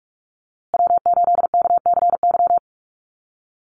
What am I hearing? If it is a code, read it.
Morse code, heard as a keyed tone, about 44 wpm, 690 Hz. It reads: W8KCY
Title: USA Call Signs Repeated 3x 45wpm